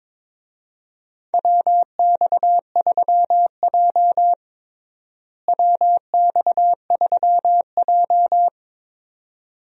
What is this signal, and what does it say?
Morse code, heard as a keyed tone, about 21 wpm, 700 Hz. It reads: WX3J WX3J
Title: USA Call Signs - 4 Characters - Repeated 2x - 22wpm